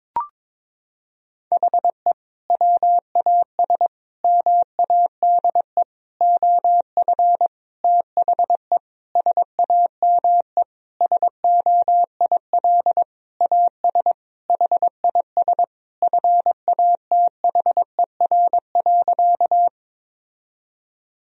Morse code, heard as a keyed tone, about 22 wpm, 700 Hz. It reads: HE WAS MADE OF THE SAME SOIL AS HIS FATHER.
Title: Sentences from Top 1000 Words - Review Part 4 - 22wpm